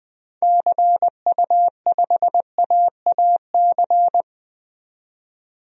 Morse code, heard as a keyed tone, about 20 wpm, 700 Hz. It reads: CU5AAC
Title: DXCC-100 Call Signs - 6 Characters - 20wpm